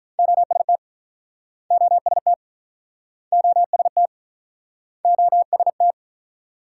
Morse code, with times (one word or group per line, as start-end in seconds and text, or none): 0.2-0.8 OST
1.7-2.3 OST
3.3-4.1 OST
5.0-5.9 OST